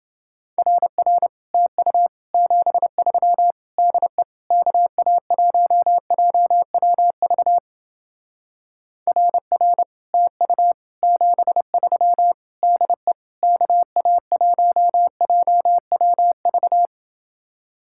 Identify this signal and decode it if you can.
Morse code, heard as a keyed tone, about 30 wpm, 700 Hz. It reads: RR TU 73 DE KA1JWV RR TU 73 DE KA1JWV